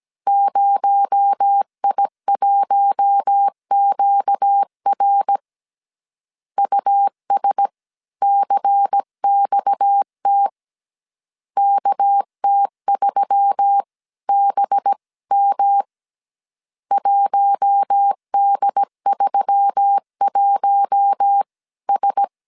Morse code, heard as a keyed tone, about 17 wpm, 800 Hz. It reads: 0I1QR USCXT KT3BM 1D31S
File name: CW / DET / 1 6 1500C S A 16wpm 2635